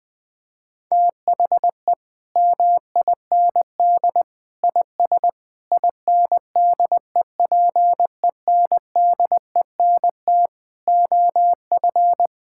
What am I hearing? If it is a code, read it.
Morse code, heard as a keyed tone, about 20 wpm, 700 Hz. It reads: THE MIND IS INDEPENDENT OF
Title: Sentences from Top 1000 Words - Review Part 6 - 20wpm